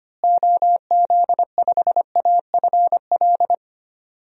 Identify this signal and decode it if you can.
Morse code, heard as a keyed tone, about 25 wpm, 700 Hz. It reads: OZ5AFL
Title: DXCC-100 Call Signs - 6 Characters - Repeated 3x - 25wpm